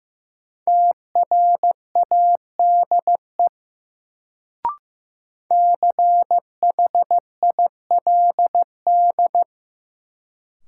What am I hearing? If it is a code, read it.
Morse code, heard as a keyed tone, about 15 wpm, 700 Hz. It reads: TRADE CHILD
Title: Sets of 5 Words Encore - New Word Drill - 15wpm